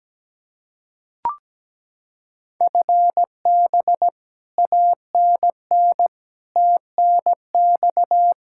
Morse code, heard as a keyed tone, about 17 wpm, 700 Hz. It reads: FB ANN TNX
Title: Practice QSO - Exchange 3 - More Info - 17wpm